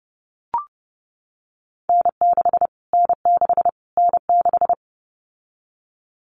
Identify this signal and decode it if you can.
Morse code, heard as a keyed tone, about 30 wpm, 690 Hz. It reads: N6 N6 N6
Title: Callsign Prefixes for Top 25 DXCC - Repeated 3x - 30wpm